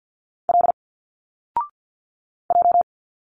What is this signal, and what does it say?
Morse code, heard as a keyed tone, about 49 wpm, 700 Hz. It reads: L J
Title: Single Letters 50wpm